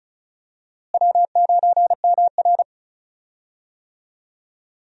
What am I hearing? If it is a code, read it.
Morse code, heard as a keyed tone, about 36 wpm, 700 Hz. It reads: W9MR
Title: USA Call Signs - 4 Characters - Repeated 3x - 35wpm